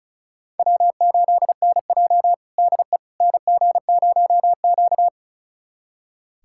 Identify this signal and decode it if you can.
Morse code, heard as a keyed tone, about 36 wpm, 700 Hz. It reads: W8NJ DE NG0Q
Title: USA Callsign DE Callsign - 4 Characters - 35wpm